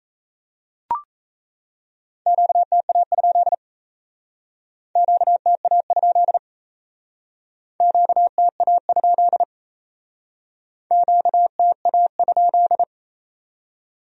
Morse code, heard as a keyed tone, about 34 wpm, 700 Hz. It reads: QTA? QTA? QTA? QTA?
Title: Speed-Racing Q-Codes - 28wpm